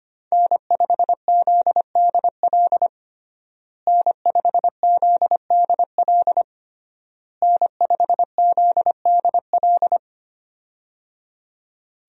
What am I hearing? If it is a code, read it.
Morse code, heard as a keyed tone, about 25 wpm, 700 Hz. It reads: N5ZDL N5ZDL N5ZDL
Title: USA Call Signs - 5 Characters - Repeated 3x - 25wpm